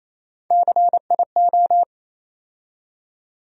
Morse code, heard as a keyed tone, about 28 wpm, 700 Hz. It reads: CIO
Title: USA Call Sign Postfixes - 3 Letters 28wpm